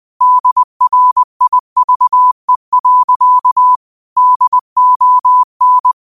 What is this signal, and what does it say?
Morse code, heard as a keyed tone, about 20 wpm, 1000 Hz. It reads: DRIVE. DON